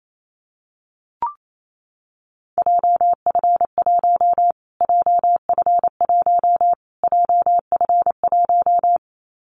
Morse code, heard as a keyed tone, about 28 wpm, 700 Hz. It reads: JF1 JF1 JF1
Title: Callsign Prefixes for Top 25 DXCC - Repeated 3x - 28wpm